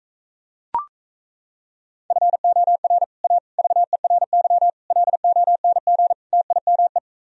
Morse code, heard as a keyed tone, about 36 wpm, 700 Hz. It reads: FOR A VERY LONG TIME